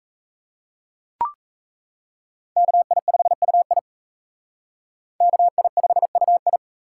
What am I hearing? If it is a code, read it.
Morse code, heard as a keyed tone, about 35 wpm, 700 Hz. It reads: KI5UI KI5UI
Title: Speed Racing - USA 2x2 Call Signs 28wpm